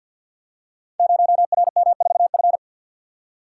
Morse code, heard as a keyed tone, about 51 wpm, 700 Hz. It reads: 0RM4F
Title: Code Groups - 5 Letter:Numbers - 50wpm